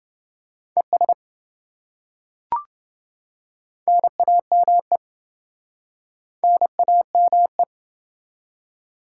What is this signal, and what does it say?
Morse code, heard as a keyed tone, about 28 wpm, 700 Hz. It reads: ES NAME NAME